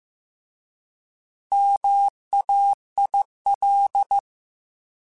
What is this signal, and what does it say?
Morse code, heard as a keyed tone, about 15 wpm, 800 Hz. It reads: MAIL
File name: Alphabetical / Alphabetical mp3s / Ms @13